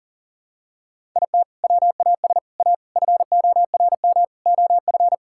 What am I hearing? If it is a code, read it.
Morse code, heard as a keyed tone, about 40 wpm, 700 Hz. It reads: IT WAS A FORM OF
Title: Sentences from Top 1000 Words - Review Part 5 - 40wpm